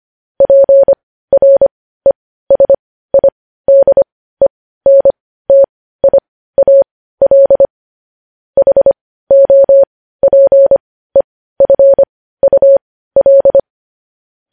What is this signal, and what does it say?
Morse code, heard as a keyed tone, about 25 wpm, spaced as for 15 wpm, 550 Hz. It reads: PRESIDENTIAL HOPEFUL
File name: podcast / News headlines-1x-25x15